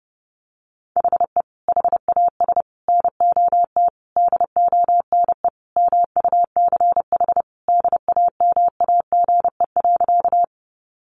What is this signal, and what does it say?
Morse code, heard as a keyed tone, about 30 wpm, 700 Hz. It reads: HE HAS NOT DONE MUCH DAMAGE.
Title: Sentences from Top 1100 Words 30wpm